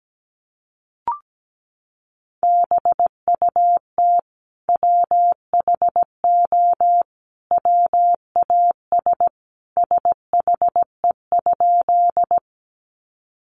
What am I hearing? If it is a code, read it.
Morse code, heard as a keyed tone, about 17 wpm, 700 Hz. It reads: BUT WHO WAS SHE?
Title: Sentences from Top 100 Words 17wpm